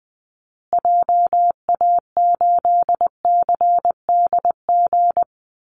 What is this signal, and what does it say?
Morse code, heard as a keyed tone, about 19 wpm, 700 Hz. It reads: JA8CDG
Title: Call Signs Repeated 2x - 20wpm